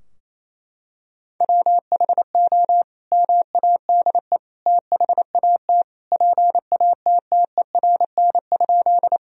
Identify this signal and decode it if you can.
Morse code, heard as a keyed tone, about 28 wpm, 700 Hz. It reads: WHO MADE THAT PATTERN?
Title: Sentences from Top 1000 Words - Review Part 2 - 28wpm